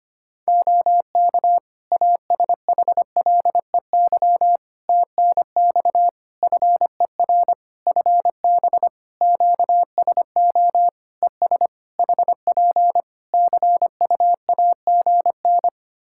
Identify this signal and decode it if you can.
Morse code, heard as a keyed tone, about 25 wpm, 700 Hz. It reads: OK ASHLEY TNX FER FB QSO ES HP CUAGN
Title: Practice QSO - Exchange 4 - Ending - 25wpm